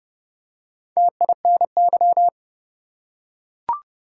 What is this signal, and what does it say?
Morse code, heard as a keyed tone, about 30 wpm, 700 Hz. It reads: TINY
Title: Four Letter Words - 30wpm